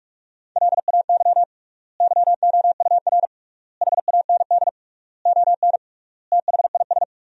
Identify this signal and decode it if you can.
Morse code, heard as a keyed tone, about 41 wpm, 700 Hz. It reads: LAY YOUR HAND ON THIS
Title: 3x Word-Spacing - Sentences from Top 400 Words - 45wpm